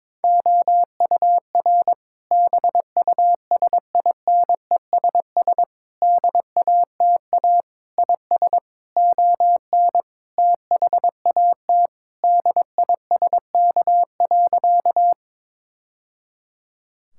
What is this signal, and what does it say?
Morse code, heard as a keyed tone, about 22 wpm, 700 Hz. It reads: OUR BUSINESS DATA IS ON THAT DISK.
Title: Sentences from Top 1000 Words - Review Part 6 - 22wpm